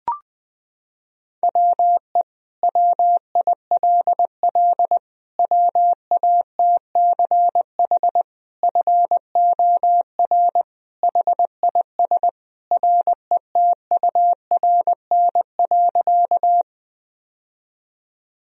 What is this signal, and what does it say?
Morse code, heard as a keyed tone, about 20 wpm, 700 Hz. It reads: WE WILL WATCH FOR HIS RETURN.
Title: Sentences from Top 1000 Words - Review Part 1 - 20wpm